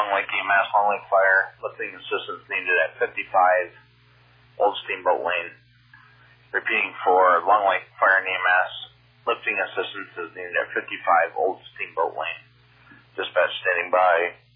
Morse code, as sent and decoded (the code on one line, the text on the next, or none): none